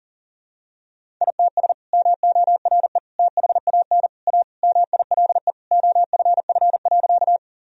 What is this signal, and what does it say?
Morse code, heard as a keyed tone, about 40 wpm, 700 Hz. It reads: ITS MORE THAN A MILE OFF.
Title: Sentences from Top 1000 Words - Review Part 1 - 40wpm